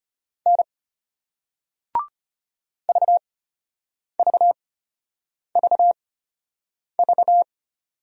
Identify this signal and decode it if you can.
Morse code, heard as a keyed tone, about 33 wpm, spaced as for 8 wpm, 700 Hz. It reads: N VVVV